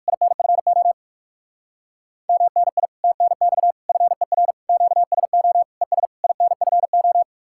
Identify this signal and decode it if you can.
Morse code, heard as a keyed tone, about 59 wpm, 700 Hz. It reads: INFO MNI TNX FER QSO ES INFO